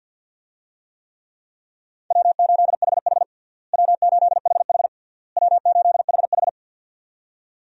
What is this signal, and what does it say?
Morse code, heard as a keyed tone, about 48 wpm, 700 Hz. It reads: W8HH W8HH W8HH
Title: USA Call Signs - 4 Characters - Repeated 3x - 50wpm